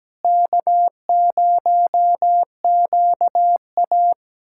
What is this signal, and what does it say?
Morse code, heard as a keyed tone, about 17 wpm, 700 Hz. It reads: K0QA
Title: USA Callsign DE Callsign - 4 Characters - 17wpm